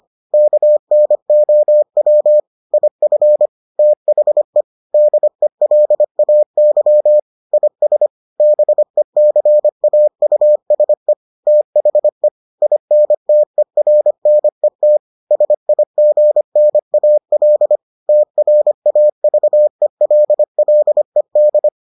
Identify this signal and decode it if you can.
Morse code, heard as a keyed tone, about 25 wpm, 600 Hz. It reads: KNOW IF THE DELAY IS BECAUSE THE INTERNET SIGNAL TRAVELLED